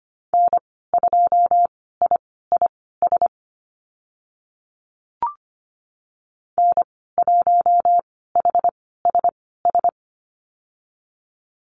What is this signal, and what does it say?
Morse code, heard as a keyed tone, about 25 wpm, spaced as for 16 wpm, 700 Hz. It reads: N2IIS N1HSS
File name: Lesson 15g - Number 5 - Call Signs 25wpm